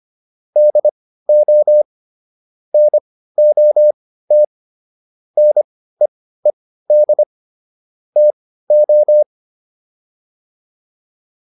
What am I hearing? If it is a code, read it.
Morse code, heard as a keyed tone, about 25 wpm, spaced as for 15 wpm, 600 Hz. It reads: DO NOT NEED TO
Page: intermediate-practice-files